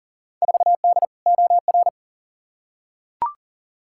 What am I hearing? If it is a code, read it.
Morse code, heard as a keyed tone, about 39 wpm, 700 Hz. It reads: 4D OR
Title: Field Day Exchanges Encore - 40wpm